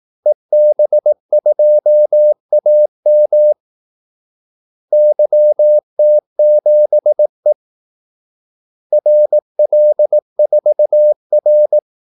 Morse code, heard as a keyed tone, about 18 wpm, 600 Hz. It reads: EB2AM YT7E RL4R
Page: Kaldesignaler 18 wpm